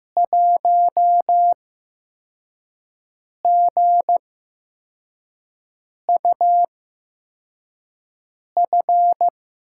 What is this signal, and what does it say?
Morse code, heard as a keyed tone, about 15 wpm, spaced as for 4 wpm, 700 Hz. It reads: 1GUF
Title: USA Call Signs - 8x Character Spacing - 15wpm